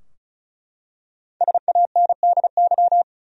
Morse code, heard as a keyed tone, about 36 wpm, 700 Hz. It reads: SANDY
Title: USA Common Names - 35wpm